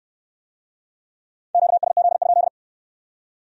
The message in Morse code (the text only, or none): QSZ?